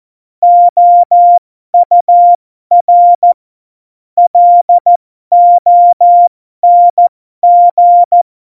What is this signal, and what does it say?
Morse code, heard as a keyed tone, about 14 wpm, spaced as for 12 wpm, 700 Hz. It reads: OUR LONG
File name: Morsekurs / Witze